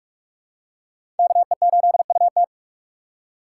Morse code, heard as a keyed tone, about 46 wpm, 700 Hz. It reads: KE8UT
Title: USA Call Signs - 2x Character Spacing - 45wpm